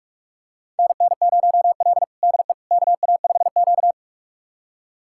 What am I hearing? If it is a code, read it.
Morse code, heard as a keyed tone, about 42 wpm, 700 Hz. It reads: NN0L DE KA5Q
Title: USA Callsign DE Callsign - 4 Characters - 45wpm